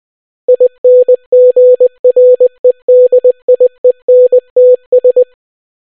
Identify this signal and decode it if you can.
Morse code, heard as a keyed tone, about 20 wpm, 500 Hz. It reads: INGREDIENTS